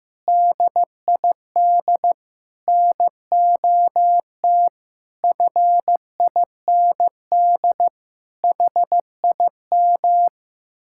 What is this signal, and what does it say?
Morse code, heard as a keyed tone, about 15 wpm, 700 Hz. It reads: DID NOT FIND HIM
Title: Sentences from Top 1600 Words 15wpm